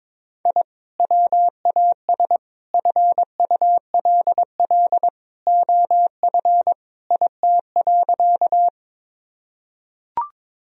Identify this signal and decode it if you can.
Morse code, heard as a keyed tone, about 22 wpm, 700 Hz. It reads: I WAS FULL OF IT.
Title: Sentences from Top 1000 Words - Review Part 3 - 22wpm